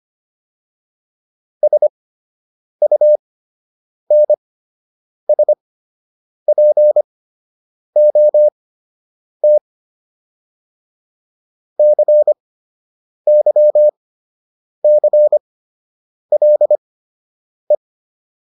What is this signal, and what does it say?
Morse code, heard as a keyed tone, about 25 wpm, spaced as for 8 wpm, 600 Hz. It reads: SUNSPOT CYCLE